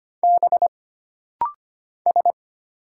Morse code, heard as a keyed tone, about 25 wpm, 700 Hz. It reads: B S
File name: Single Character (Letter:Number:Punctuation:Prosign) - Rapid-Fire - 25wpm